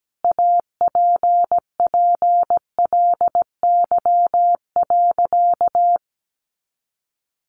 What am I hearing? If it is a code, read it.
Morse code, heard as a keyed tone, about 17 wpm, 700 Hz. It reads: APPLY.
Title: Sentences from Top 1400 Words 17wpm